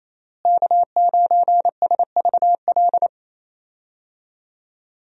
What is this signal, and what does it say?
Morse code, heard as a keyed tone, about 28 wpm, 700 Hz. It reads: K9SVL
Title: USA Call Signs - 5 Characters - Repeated 3x - 28wpm